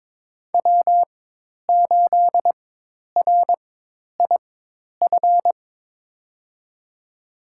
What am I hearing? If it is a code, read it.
Morse code, heard as a keyed tone, about 22 wpm, spaced as for 10 wpm, 700 Hz. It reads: W8RIF
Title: USA Call Signs - 4x Character Spacing - 22wpm